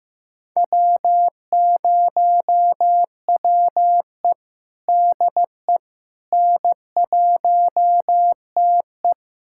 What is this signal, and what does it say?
Morse code, heard as a keyed tone, about 16 wpm, 700 Hz. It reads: W0WE DE N1TE
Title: USA Callsign DE Callsign - 4 Characters - 15wpm